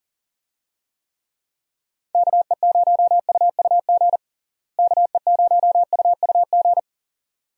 Morse code, heard as a keyed tone, about 40 wpm, 700 Hz. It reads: KE0UUG KE0UUG
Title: USA Call Signs - 6 Characters - Repeated 2x - 40wpm